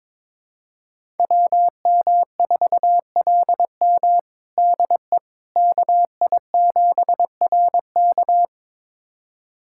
Morse code, heard as a keyed tone, about 22 wpm, 700 Hz. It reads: WM4LM DE KI7RK
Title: USA Callsign DE Callsign - 5 Characters - 22wpm